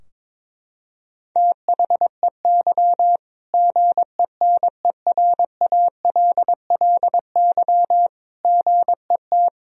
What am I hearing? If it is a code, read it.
Morse code, heard as a keyed tone, about 22 wpm, 700 Hz. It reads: THEY GENERALLY GET
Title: Sentences from Top 1000 Words - Review Part 4 - 22wpm